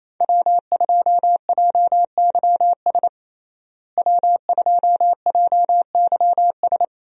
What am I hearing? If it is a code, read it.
Morse code, heard as a keyed tone, about 28 wpm, 700 Hz. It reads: W2JYS W2JYS
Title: USA Call Signs - 5 Characters - Repeated 2x - 28wpm